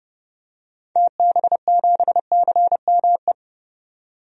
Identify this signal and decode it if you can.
Morse code, heard as a keyed tone, about 30 wpm, 700 Hz. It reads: TB7CME
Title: DXCC-100 Call Signs - 6 Characters - 30wpm